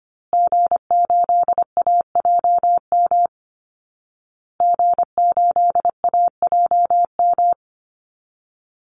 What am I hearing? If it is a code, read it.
Morse code, heard as a keyed tone, about 25 wpm, 700 Hz. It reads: G8AJM G8AJM
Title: Call Signs Repeated 2x - 25wpm